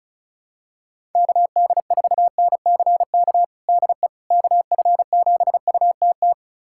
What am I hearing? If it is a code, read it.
Morse code, heard as a keyed tone, about 36 wpm, 700 Hz. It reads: KD4NCK DE KF7UTT
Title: USA Callsign DE Callsign 35wpm